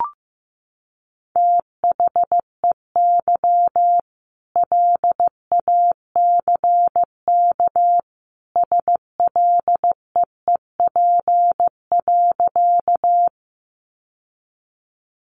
Morse code, heard as a keyed tone, about 15 wpm, 700 Hz. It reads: THEY LACK SLEEP.
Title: Sentences from Top 1100 Words 15wpm